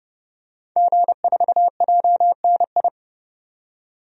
Morse code, heard as a keyed tone, about 30 wpm, 690 Hz. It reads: G4JNI
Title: DXCC-100 Call Signs - 5 Characters - 30wpm